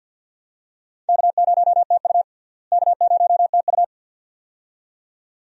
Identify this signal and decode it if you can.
Morse code, heard as a keyed tone, about 50 wpm, 700 Hz. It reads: K0TU K0TU
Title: USA Call Signs - 4 Characters - Repeated 2x - 50wpm